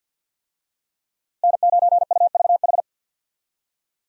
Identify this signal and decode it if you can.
Morse code, heard as a keyed tone, about 52 wpm, 700 Hz. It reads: N9UVH
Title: USA Call Signs - 5 Characters - Repeated 3x - 50wpm